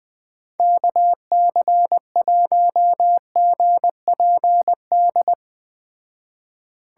Morse code, heard as a keyed tone, about 20 wpm, 700 Hz. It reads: KC1GPD